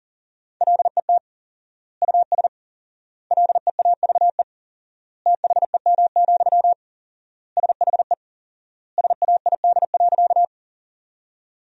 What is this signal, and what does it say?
Morse code, heard as a keyed tone, about 37 wpm, 700 Hz. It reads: LET US LEAVE THEM, SHE SAID.
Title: 4x Word-Spacing - Sentences from Top 400 Words - 40wpm